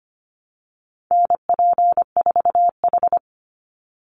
Morse code, heard as a keyed tone, about 25 wpm, 700 Hz. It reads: NP4H
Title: Call Signs Repeated 3x - 25wpm